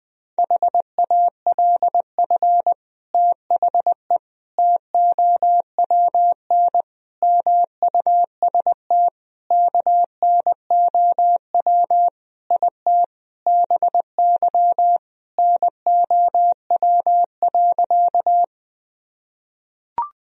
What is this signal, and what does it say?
Morse code, heard as a keyed tone, about 20 wpm, 700 Hz. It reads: HALF THE TOWN MUST KNOW IT BY NOW.
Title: Sentences from Top 1000 Words - Review Part 2 - 20wpm